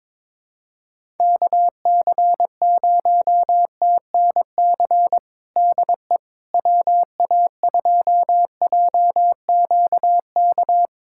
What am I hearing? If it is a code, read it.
Morse code, heard as a keyed tone, about 22 wpm, 700 Hz. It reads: KC0TNC DE WA2JQK
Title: USA Callsign DE Callsign - 6 Characters - 22wpm